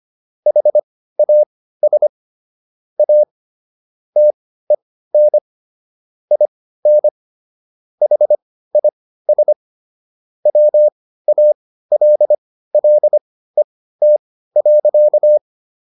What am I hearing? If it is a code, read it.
Morse code, heard as a keyed tone, about 25 wpm, spaced as for 15 wpm, 600 Hz. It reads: HAS A TEN IN HIS WALLET.